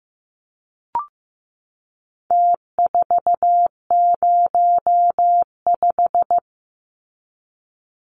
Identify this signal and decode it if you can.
Morse code, heard as a keyed tone, about 15 wpm, 700 Hz. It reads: T405
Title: Numbers - Contest Style - 4-Digit 15wpm